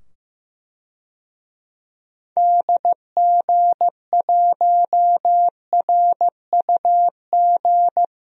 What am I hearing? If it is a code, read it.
Morse code, heard as a keyed tone, about 15 wpm, 700 Hz. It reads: DG1RUG
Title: DXCC-100 Call Signs - 6 Characters - 15wpm